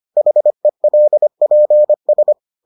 Morse code, heard as a keyed tone, about 25 wpm, 600 Hz. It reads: HELPS